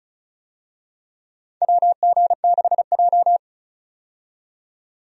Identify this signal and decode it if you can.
Morse code, heard as a keyed tone, about 36 wpm, 700 Hz. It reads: WG6J
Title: USA Call Signs - 4 Characters - 35wpm